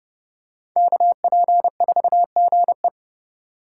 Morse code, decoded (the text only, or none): KP4GE